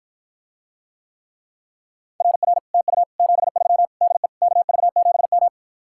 Morse code, heard as a keyed tone, about 54 wpm, 700 Hz. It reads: RR TU 73 DE KV7M